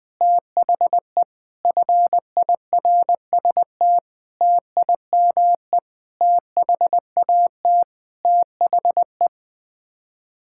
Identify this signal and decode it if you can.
Morse code, heard as a keyed tone, about 20 wpm, 700 Hz. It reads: THE FIRST TIME THAT THE
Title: Sets of 8 Words - 20wpm